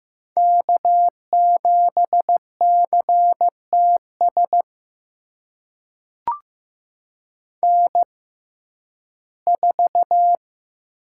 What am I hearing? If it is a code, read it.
Morse code, heard as a keyed tone, about 16 wpm, spaced as for 11 wpm, 700 Hz. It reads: K7CTS N4
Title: USA Call Signs - 6x Character Spacing - 15wpm